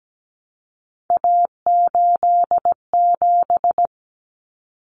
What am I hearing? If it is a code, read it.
Morse code, heard as a keyed tone, about 17 wpm, 700 Hz. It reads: A87
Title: Common Cut Numbers - 3-Digit 17wpm